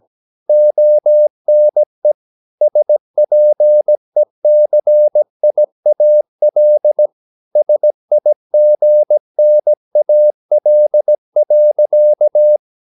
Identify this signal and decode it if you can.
Morse code, heard as a keyed tone, about 17 wpm, 600 Hz. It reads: ONE SPECIAL SIGNAL.
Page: Romanen Never af Ken Follet på engelsk.